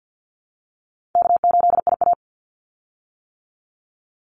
Morse code, heard as a keyed tone, about 48 wpm, 700 Hz. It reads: K8IA
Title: Call Signs - 50wpm